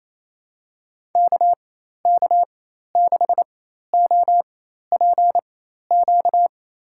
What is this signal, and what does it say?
Morse code, heard as a keyed tone, about 28 wpm, spaced as for 13 wpm, 700 Hz. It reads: KK6OPQ
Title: USA Call Signs - 6 Characters - 4x Character Spacing - 28wpm